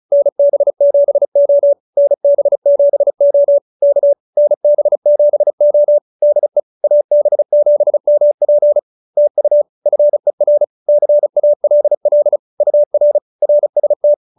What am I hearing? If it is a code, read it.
Morse code, heard as a keyed tone, about 34 wpm, 590 Hz. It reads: NB7O NB7O K NB7O DE AB7MP TU FER CALL UR RST